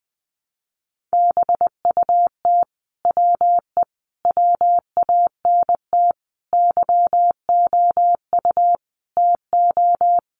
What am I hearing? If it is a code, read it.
Morse code, heard as a keyed tone, about 20 wpm, 700 Hz. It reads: BUT WE WANT YOU TO
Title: Sentences from Top 700 Words 20wpm